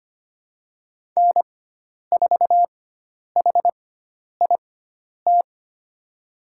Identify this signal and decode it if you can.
Morse code, heard as a keyed tone, about 25 wpm, spaced as for 10 wpm, 700 Hz. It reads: N4HIT
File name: Lesson 16g - Letter c - Call Signs 25wpm